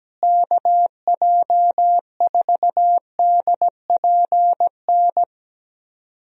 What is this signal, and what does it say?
Morse code, heard as a keyed tone, about 17 wpm, 700 Hz. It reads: KJ4DPN
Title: USA Call Signs - 6 Characters - Repeated 3x - 17wpm